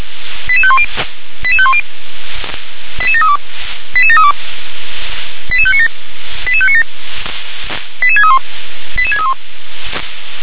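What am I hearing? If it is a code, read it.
Five-tone selective call (ZVEI), heard as quick runs of five tones, tones of about 70 ms. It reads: 88510 88510 88732 88732 88597 88597 88631 88631
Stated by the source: ZVEI five-tone call received by SDR.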